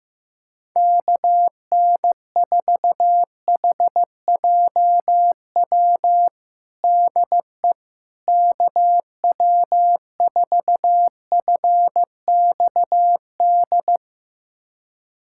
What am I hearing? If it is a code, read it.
Morse code, heard as a keyed tone, about 16 wpm, 700 Hz. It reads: KN4HJW DE KW4FXD
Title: USA Callsign DE Callsign - 6 Characters - 15wpm